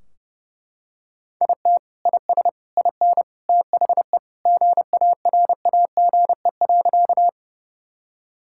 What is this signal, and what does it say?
Morse code, heard as a keyed tone, about 30 wpm, 700 Hz. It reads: IT IS IN THE GARAGE.
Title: Sentences from Top 1600 Words 30wpm